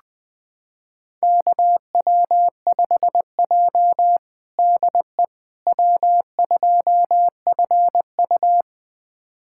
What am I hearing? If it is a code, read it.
Morse code, heard as a keyed tone, about 20 wpm, 700 Hz. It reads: KW5J DE W2FU
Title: USA Callsign DE Callsign - 4 Characters - 20wpm